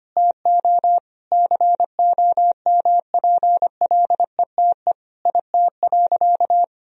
Morse code, heard as a keyed tone, about 25 wpm, 700 Hz. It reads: TO COMPLETE IT.
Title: Sentences from Top 1000 Words - Review Part 3 - 25wpm